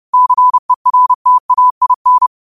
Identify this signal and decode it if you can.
Morse code, heard as a keyed tone, about 30 wpm, 1000 Hz. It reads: CERTAIN
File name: qotdfiles / 30wpm